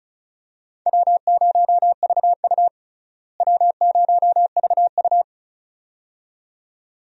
Morse code, heard as a keyed tone, about 36 wpm, 700 Hz. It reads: W0VU W0VU
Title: USA Call Signs - 4 Characters - Repeated 2x - 35wpm